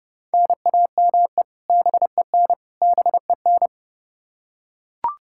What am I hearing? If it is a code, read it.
Morse code, heard as a keyed tone, about 30 wpm, 700 Hz. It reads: NAME BEN BEN
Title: K1USN SST QSO Practice 30wpm